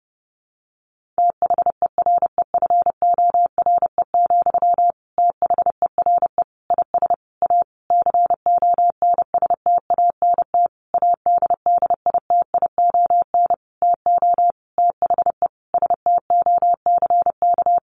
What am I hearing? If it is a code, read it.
Morse code, heard as a keyed tone, about 30 wpm, 700 Hz. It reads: THEREFORE, THERE IS A CONSTANT ADDITION TO THE STOCK
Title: Sentences from Top 1500 Words 30wpm